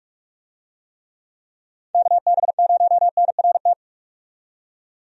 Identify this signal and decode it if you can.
Morse code, heard as a keyed tone, about 46 wpm, 700 Hz. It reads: KB0NRT
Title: USA Call Signs - 6 Characters - 45wpm